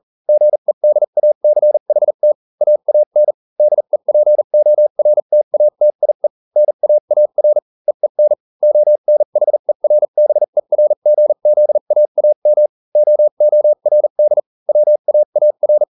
Morse code, heard as a keyed tone, about 40 wpm, spaced as for 35 wpm, 600 Hz. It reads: GEDACHT AAN DEPORTATIE NAAR EEN ONHERBERGZAAM OORD WAAR